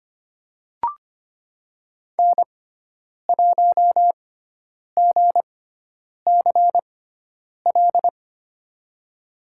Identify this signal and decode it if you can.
Morse code, heard as a keyed tone, about 25 wpm, spaced as for 9 wpm, 700 Hz. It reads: N1GCL
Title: USA Call Signs - 5 Characters - 6x Character Spacing - 25wpm